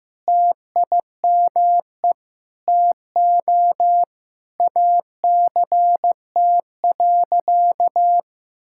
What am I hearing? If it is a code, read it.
Morse code, heard as a keyed tone, about 15 wpm, 700 Hz. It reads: TIME TO ACT.
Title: Sentences from Top 1000 Words - Review Part 6 - 15wpm